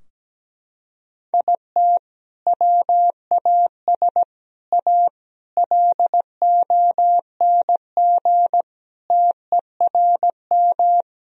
Morse code, heard as a keyed tone, about 17 wpm, 700 Hz. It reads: IT WAS A LONG TERM
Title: Sentences from Top 1000 Words - Review Part 4 - 17wpm